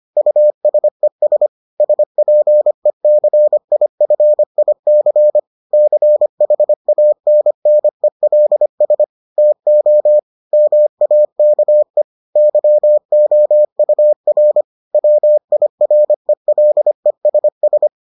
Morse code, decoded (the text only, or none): USES SPECIFIC CHANNELS TO MAKE YOUR WIRELESS